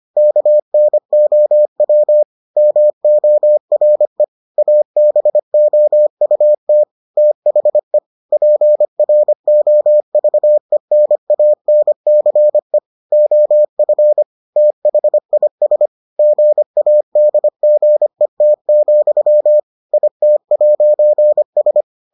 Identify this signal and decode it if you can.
Morse code, heard as a keyed tone, about 25 wpm, 600 Hz. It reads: KNOW MORE ABOUT THE PROVENANCE OF THIS GADGET, IT'S